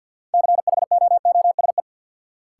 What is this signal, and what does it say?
Morse code, heard as a keyed tone, about 53 wpm, 700 Hz. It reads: CHOOSE
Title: Sentences from Top 700 Words - New Word Drill - 50wpm